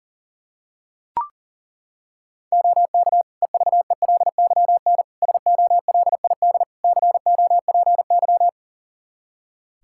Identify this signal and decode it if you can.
Morse code, heard as a keyed tone, about 37 wpm, 700 Hz. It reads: OK EVELYN SOLID COPY
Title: Practice QSO - Exchange 3 - More Info - 40wpm